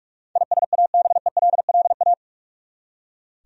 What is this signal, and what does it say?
Morse code, heard as a keyed tone, about 46 wpm, 700 Hz. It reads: ISABELLA
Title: USA Common Names - 45wpm